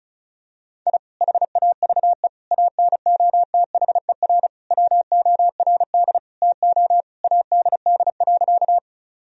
Morse code, heard as a keyed tone, about 36 wpm, 700 Hz. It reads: I HAVE ANOTHER WORD TO ADD.
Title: Sentences from Top 1000 Words - Review Part 1 - 35wpm